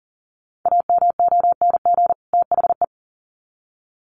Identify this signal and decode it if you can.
Morse code, heard as a keyed tone, about 39 wpm, 700 Hz. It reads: AMONG THE
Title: Sets of 2 Words 40wpm